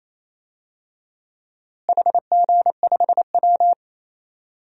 Morse code, heard as a keyed tone, about 27 wpm, 700 Hz. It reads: HG5W
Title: DXCC-100 Call Signs - 4 Characters - Repeated 2x - 28wpm